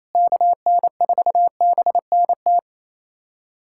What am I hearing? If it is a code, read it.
Morse code, heard as a keyed tone, about 28 wpm, 700 Hz. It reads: KN4BNT
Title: USA Callsign DE Callsign - 6 Characters - 28wpm